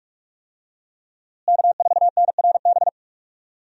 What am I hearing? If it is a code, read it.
Morse code, heard as a keyed tone, about 46 wpm, 700 Hz. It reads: K4NRB
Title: USA Call Signs - 5 Characters - Repeated 3x - 45wpm